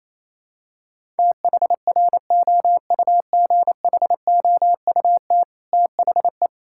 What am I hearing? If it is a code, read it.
Morse code, heard as a keyed tone, about 28 wpm, 700 Hz. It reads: THROUGHOUT THE
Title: Sentences from Top 1000 Words - Review Part 5 - 28wpm